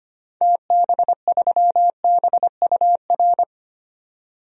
Morse code, heard as a keyed tone, about 25 wpm, 700 Hz. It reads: TB3BUR
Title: DXCC-100 Call Signs - 6 Characters - 25wpm